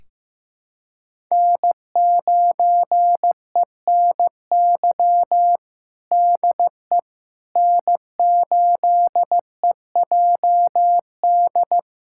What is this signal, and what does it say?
Morse code, heard as a keyed tone, about 15 wpm, 700 Hz. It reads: N9ENY DE N8EJD
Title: USA Callsign DE Callsign - 5 Characters - 15wpm